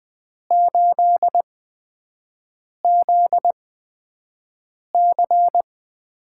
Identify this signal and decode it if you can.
Morse code, heard as a keyed tone, about 20 wpm, spaced as for 5 wpm, 700 Hz. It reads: 8ZC
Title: USA Call Signs - 5 Characters - 8x Character Spacing - 20wpm